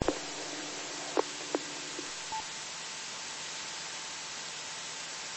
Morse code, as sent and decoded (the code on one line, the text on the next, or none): none